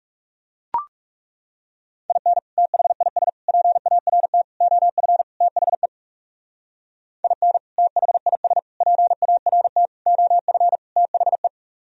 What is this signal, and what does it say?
Morse code, heard as a keyed tone, about 36 wpm, 700 Hz. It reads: IN THIS PART OF THE IN THIS PART OF THE
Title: Speed-Racing - Sets of 5 Words 30wpm